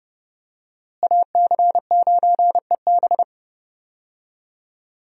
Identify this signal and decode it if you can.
Morse code, heard as a keyed tone, about 30 wpm, 700 Hz. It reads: AC9EB
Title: USA Call Signs - 5 Characters - Repeated 3x - 30wpm